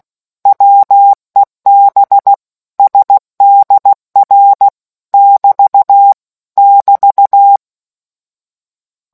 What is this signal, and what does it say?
Morse code, heard as a keyed tone, about 16 wpm, 800 Hz. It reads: WEB SDR <BT> <BT>